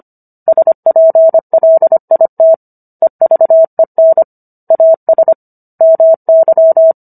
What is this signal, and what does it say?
Morse code, heard as a keyed tone, about 25 wpm, 650 Hz. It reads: SPLIT EVEN AS MY